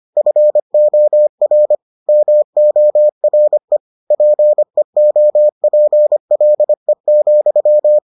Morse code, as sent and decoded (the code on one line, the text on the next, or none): ..-. --- .-. / -- --- .-. / .--. --- .--. .-.. --..--
FOR MORE PEOPLE,